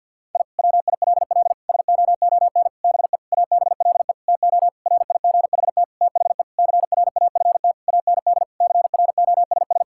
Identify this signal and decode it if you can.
Morse code, heard as a keyed tone, about 50 wpm, 700 Hz. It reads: I WILL SOON BE ABLE TO RIGHT THE CRAFT AND CROSS
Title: Sentences from Top 1000 Words - Review Part 6 - 50wpm